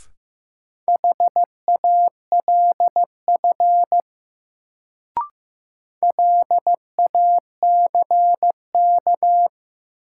Morse code, heard as a keyed tone, about 15 wpm, 700 Hz. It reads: HALF LACK
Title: Sets of 3 Words - New Word Drill - 15wpm